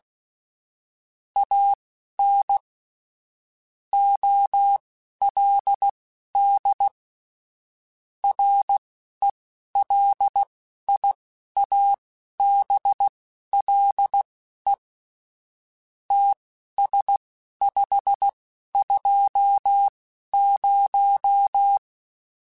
Morse code, heard as a keyed tone, about 16 wpm, spaced as for 11 wpm, 800 Hz. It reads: AN OLD RELIABLE TS520
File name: QSOs / QSOs-mp3s / WD0FBT 1@10